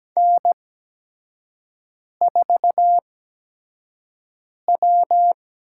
Morse code, heard as a keyed tone, about 17 wpm, spaced as for 5 wpm, 700 Hz. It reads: N4W
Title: USA Call Signs - 5 Characters - 8x Character Spacing - 17wpm